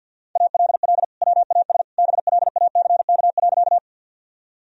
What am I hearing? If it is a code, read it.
Morse code, heard as a keyed tone, about 51 wpm, 700 Hz. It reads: ALL WAS BLACK.
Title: Sentences from Top 1000 Words - Review Part 2 - 50wpm